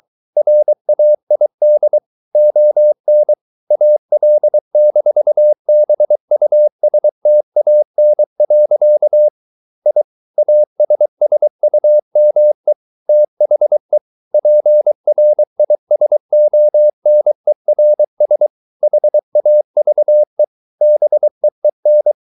Morse code, heard as a keyed tone, about 23 wpm, 600 Hz. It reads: RAID ON AL-BUSTAN. I ASSUME THE PRISONERS HAVE BEEN